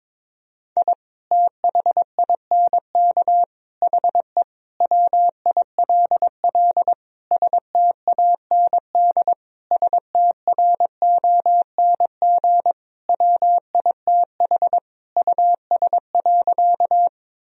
Morse code, heard as a keyed tone, about 22 wpm, 700 Hz. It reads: I THINK HE WILL STAND STRONG WITH US.
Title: Sentences from Top 1000 Words - Review Part 3 - 22wpm